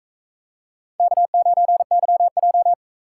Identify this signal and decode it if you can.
Morse code, heard as a keyed tone, about 39 wpm, 700 Hz. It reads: K9YJ